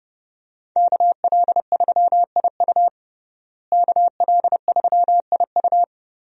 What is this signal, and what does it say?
Morse code, heard as a keyed tone, about 30 wpm, 700 Hz. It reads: KL3IU KL3IU
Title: USA Call Signs - 5 Characters - Repeated 3x - 30wpm